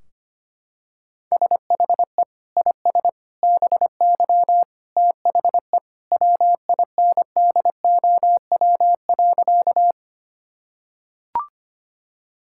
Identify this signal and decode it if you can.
Morse code, heard as a keyed tone, about 25 wpm, 700 Hz. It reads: SHE IS BY THE WINDOW.
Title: Sentences from Top 1600 Words 25wpm